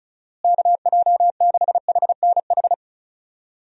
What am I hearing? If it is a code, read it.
Morse code, heard as a keyed tone, about 36 wpm, 700 Hz. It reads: KJ6HNH
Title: USA Call Signs - 6 Characters - Repeated 3x - 35wpm